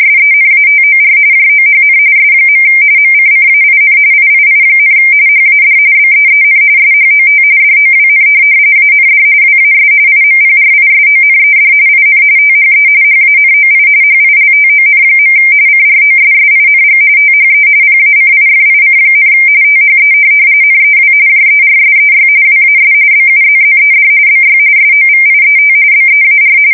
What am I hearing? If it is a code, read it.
RTTY, about 45.45 baud. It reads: SENATOR JOHN S. MCCAIN III, HAD BEEN SHOT DOWN AND WAS IMPRISONED IN THE HANOI HILTON. CROSSING THE PACIFIC, THE NEW JERSEY WAS BUZZED BY SOVIET BEAR
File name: fsk170 / USS-New-Jersey